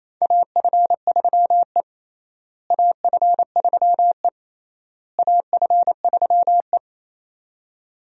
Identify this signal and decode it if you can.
Morse code, heard as a keyed tone, about 28 wpm, 700 Hz. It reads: AF3E AF3E AF3E
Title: USA Call Signs - 4 Characters - Repeated 3x - 28wpm